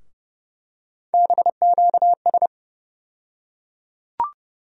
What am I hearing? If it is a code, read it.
Morse code, heard as a keyed tone, about 31 wpm, 700 Hz. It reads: BQS
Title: USA Call Sign Postfixes - 3 Letters 30wpm